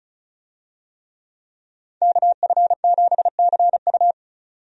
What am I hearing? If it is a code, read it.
Morse code, heard as a keyed tone, about 34 wpm, 700 Hz. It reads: KF7CU
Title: USA Call Signs - 5 Characters - 35wpm